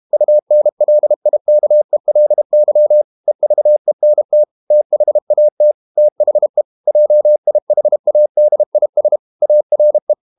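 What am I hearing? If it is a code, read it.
Morse code, heard as a keyed tone, about 32 wpm, 600 Hz. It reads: UNLIKELY EVENT THAT THE JIHADIS ARE
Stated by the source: Morse code practice at 32 wpm from CW med Gnister, reading Ken Follett's novel Never in English.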